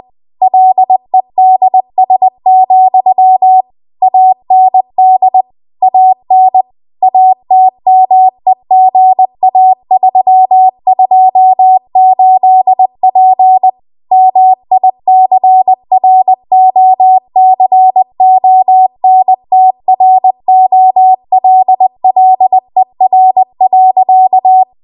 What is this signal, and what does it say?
Morse code, heard as a keyed tone, about 20 wpm, 750 Hz. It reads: LEDS, AND AN ATMEGA328P MICROCONTROLLER.